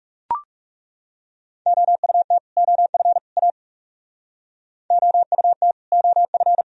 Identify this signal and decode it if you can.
Morse code, heard as a keyed tone, about 36 wpm, 700 Hz. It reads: OUT OF A OUT OF
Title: Speed-Racing - Sets of 3 Words 30wpm